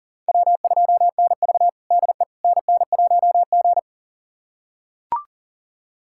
Morse code, heard as a keyed tone, about 40 wpm, 700 Hz. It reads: W2NV DE NN1G
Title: USA Callsign DE Callsign - 4 Characters - 40wpm